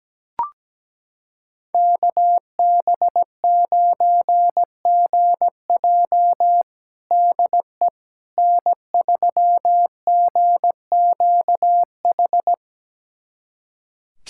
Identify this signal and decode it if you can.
Morse code, heard as a keyed tone, about 17 wpm, 700 Hz. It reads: KB9GJ DE N3GQH
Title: USA Callsign DE Callsign - 5 Characters - 17wpm